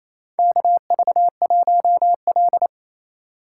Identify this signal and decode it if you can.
Morse code, heard as a keyed tone, about 28 wpm, 700 Hz. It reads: KV1L
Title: USA Call Signs - 4 Characters - 3x Character Spacing - 28wpm